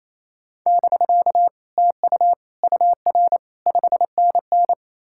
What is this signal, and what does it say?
Morse code, heard as a keyed tone, about 37 wpm, 700 Hz. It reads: <BK> TU UR 5NN